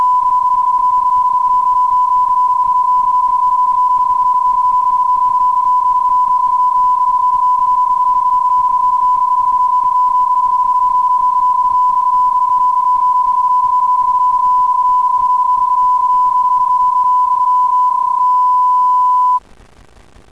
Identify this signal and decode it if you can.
PSK31, about 31.25 baud.